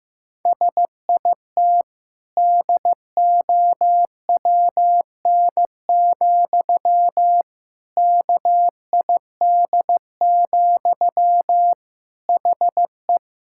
Morse code, heard as a keyed tone, about 15 wpm, 700 Hz. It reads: SIT DOWN, KID, HE
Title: Sentences from Top 1600 Words 15wpm